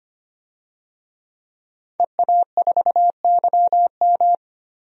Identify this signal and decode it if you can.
Morse code, heard as a keyed tone, about 25 wpm, 700 Hz. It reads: EA4YM
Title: DXCC-100 Call Signs - 5 Characters - 25wpm